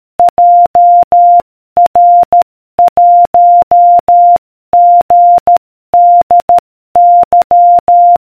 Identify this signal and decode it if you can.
Morse code, heard as a keyed tone, about 13 wpm, 700 Hz. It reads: JR1GDY